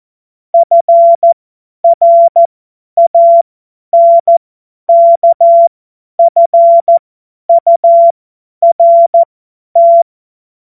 Morse code, heard as a keyed tone, about 14 wpm, spaced as for 10 wpm, 680 Hz. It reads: FRANKFURT
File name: Morsekurs / Witze / Witz 25. April 2023 50 F70